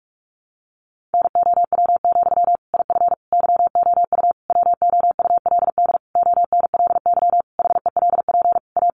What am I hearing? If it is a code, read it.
Morse code, heard as a keyed tone, about 44 wpm, 700 Hz. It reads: NOW, IF YOU WOULD ONLY HELP A
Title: Sentences from Top 900 Words 45wpm